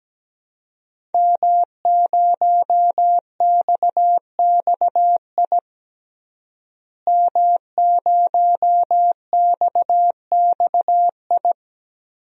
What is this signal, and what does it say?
Morse code, heard as a keyed tone, about 17 wpm, 700 Hz. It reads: M0XXI M0XXI
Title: DXCC-100 Call Signs - 5 Characters - Repeated 3x - 17wpm